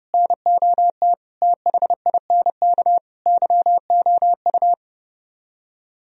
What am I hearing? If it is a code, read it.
Morse code, heard as a keyed tone, about 35 wpm, 700 Hz. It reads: NOT THINK YOU
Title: Speed-Racing - Sets of 4 Words 30wpm